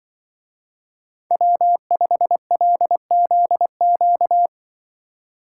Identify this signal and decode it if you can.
Morse code, heard as a keyed tone, about 23 wpm, 700 Hz. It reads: W5LZQ
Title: Speed Racing - USA 1x3 Call Signs 20wpm